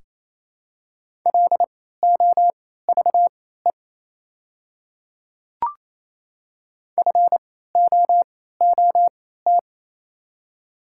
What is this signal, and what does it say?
Morse code, heard as a keyed tone, about 28 wpm, spaced as for 16 wpm, 700 Hz. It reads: LOVE FOOT
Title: 3x Character Spacing - Top 500 Words - 28wpm